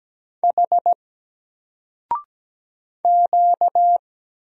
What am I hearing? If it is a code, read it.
Morse code, heard as a keyed tone, about 17 wpm, 700 Hz. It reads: H Q
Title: Single Character (Letter:Number:Punctuation:Prosign) 17wpm